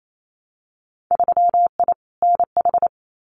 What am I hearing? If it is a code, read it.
Morse code, heard as a keyed tone, about 28 wpm, 700 Hz. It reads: 3I NH
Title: Winter Field Day Exchanges Encore 28wpm